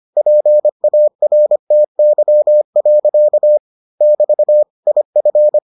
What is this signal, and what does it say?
Morse code, heard as a keyed tone, about 25 wpm, 600 Hz. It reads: PARTY. <BT> IF